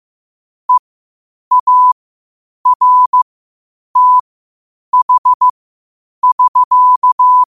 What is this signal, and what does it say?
Morse code, heard as a keyed tone, about 15 wpm, spaced as for 8 wpm, 1000 Hz. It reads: EARTH<SK>
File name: qotdfiles / 8wpm